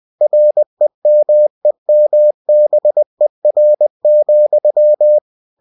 Morse code, heard as a keyed tone, about 20 wpm, 600 Hz. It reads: REMEMBER,
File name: cw-books / winnie-the-pooh / winnie-20wpm-page-0011